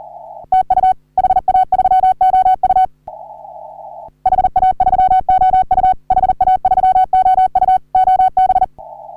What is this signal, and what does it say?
Morse code, heard as a keyed tone, about 40 wpm, 750 Hz. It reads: TU HA3OU HA3OU HA3OU OB
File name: CW / HA3OU RHOBCW REC